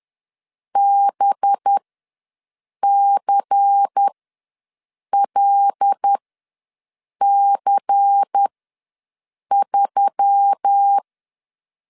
Morse code, heard as a keyed tone, about 10 wpm, spaced as for 6 wpm, 800 Hz. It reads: BCLC3
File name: LEZIONI / 5 LBVC / 5 LBVC ws 10wpm 3733